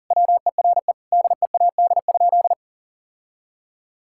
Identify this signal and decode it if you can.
Morse code, heard as a keyed tone, about 37 wpm, 700 Hz. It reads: WERE DEAD?
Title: Sentences from Top 1600 Words 40wpm